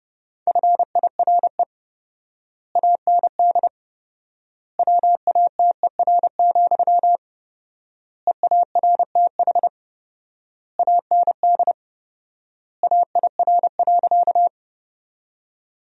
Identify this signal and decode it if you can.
Morse code, heard as a keyed tone, about 30 wpm, 700 Hz. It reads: FIRE AND WATER, EARTH AND AIR.
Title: 4x Word-Spacing - Sentences from Top 400 Words - 30wpm